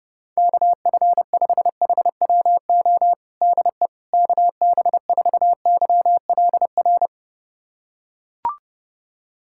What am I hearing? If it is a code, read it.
Morse code, heard as a keyed tone, about 32 wpm, 700 Hz. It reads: KF5HWO DE KB4YLR
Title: USA Callsign DE Callsign - 6 Characters - 30wpm